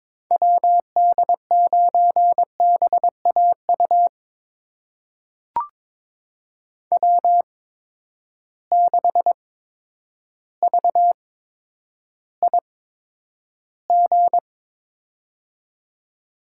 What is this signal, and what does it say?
Morse code, heard as a keyed tone, about 22 wpm, spaced as for 10 wpm, 700 Hz. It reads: WD9BAU W6VIG
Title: USA Call Signs - 8x Character Spacing - 22wpm